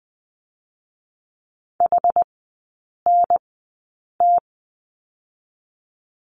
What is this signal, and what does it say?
Morse code, heard as a keyed tone, about 19 wpm, spaced as for 8 wpm, 700 Hz. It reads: HNT
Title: 060 Lesson 11e - Letter h - Three Characters 20wpm